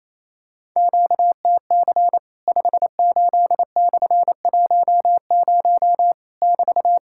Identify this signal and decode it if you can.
Morse code, heard as a keyed tone, about 28 wpm, 700 Hz. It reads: QTC 58/10 <BT>
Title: Worked All Europe DX Contest - QTC Exchange - 28wpm